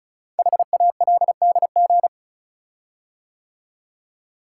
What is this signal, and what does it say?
Morse code, heard as a keyed tone, about 36 wpm, 700 Hz. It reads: HALDG
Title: Code Groups - 5 Letters - 35wpm